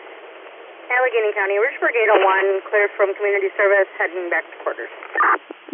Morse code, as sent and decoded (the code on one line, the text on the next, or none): none